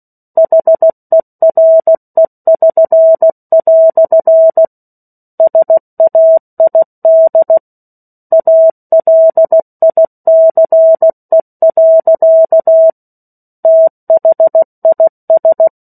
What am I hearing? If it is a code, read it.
Morse code, heard as a keyed tone, about 16 wpm, 650 Hz. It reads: HERE<SN>" SAID ALICE. THIS
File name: 03-Chapter-03 16 WPM